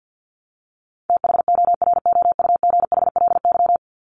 Morse code, heard as a keyed tone, about 50 wpm, 700 Hz. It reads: THOROUGHLY
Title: LY Suffix - Top 100 Words with Base Words 50wpm